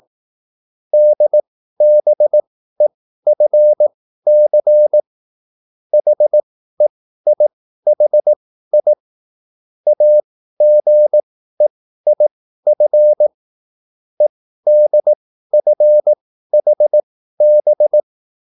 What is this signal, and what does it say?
Morse code, heard as a keyed tone, about 18 wpm, spaced as for 13 wpm, 600 Hz. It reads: DBEFC HEIHI AGEIF EDFHB
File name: QSL / CW lesson